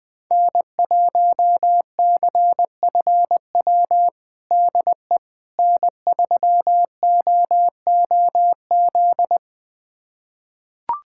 Morse code, heard as a keyed tone, about 21 wpm, 700 Hz. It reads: N1CFW DE N3OOZ
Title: USA Callsign DE Callsign 20wpm